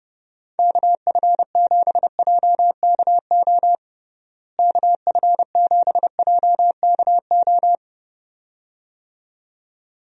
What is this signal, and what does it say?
Morse code, heard as a keyed tone, about 32 wpm, 700 Hz. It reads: KF7JKO KF7JKO
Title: USA Call Signs - 6 Characters - Repeated 2x - 30wpm